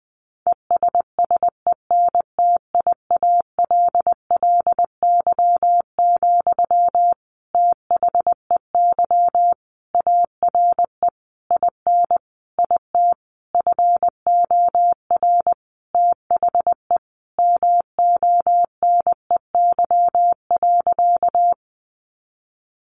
Morse code, heard as a keyed tone, about 20 wpm, 700 Hz. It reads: ESSENTIALLY, THEY ARE IN IT FOR THE MONEY.
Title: Sentences from Top 1300 Words 20wpm